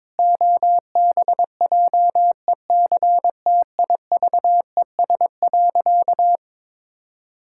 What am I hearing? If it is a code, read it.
Morse code, heard as a keyed tone, about 22 wpm, 700 Hz. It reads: OBJECTIVES.
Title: Pangram Challenge v2 - 22wpm